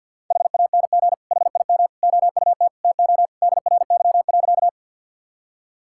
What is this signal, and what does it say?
Morse code, heard as a keyed tone, about 50 wpm, 700 Hz. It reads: HANG HIM OUT TO DRY.
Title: Sentences from Top 1600 Words 50wpm